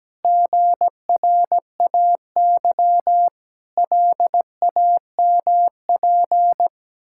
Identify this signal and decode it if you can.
Morse code, heard as a keyed tone, about 17 wpm, 700 Hz. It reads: GRAY LAMP
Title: Pangram Challenge v2 - 17wpm